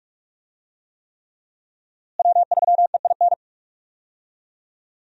Morse code, heard as a keyed tone, about 45 wpm, 700 Hz. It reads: W3EIN
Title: USA Call Signs - 5 Characters - Repeated 3x - 45wpm